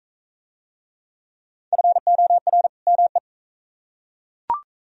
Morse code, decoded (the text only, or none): FOR ME